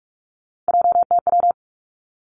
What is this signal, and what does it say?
Morse code, heard as a keyed tone, about 46 wpm, 700 Hz. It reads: JTW
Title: Three Challenging Letter-Number Combinations 45wpm